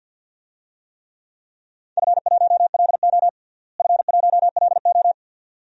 Morse code, heard as a keyed tone, about 52 wpm, 700 Hz. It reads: F1LO F1LO
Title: DXCC-100 Call Signs - 4 Characters - Repeated 2x - 50wpm